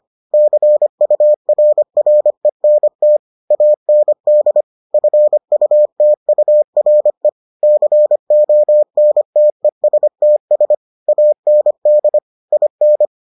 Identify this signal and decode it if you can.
Morse code, heard as a keyed tone, about 25 wpm, 600 Hz. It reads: CURRENT AND FUTURE CONTESTS AND IN